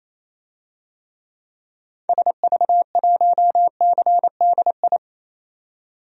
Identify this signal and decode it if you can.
Morse code, heard as a keyed tone, about 28 wpm, 700 Hz. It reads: SV1CDI
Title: DXCC-100 Call Signs - 6 Characters - 28wpm